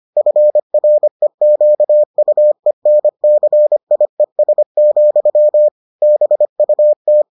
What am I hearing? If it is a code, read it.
Morse code, heard as a keyed tone, about 25 wpm, 600 Hz. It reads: FREQUENCIES, BUT